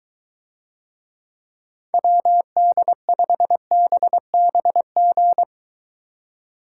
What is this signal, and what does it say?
Morse code, heard as a keyed tone, about 23 wpm, 700 Hz. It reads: WD5BBG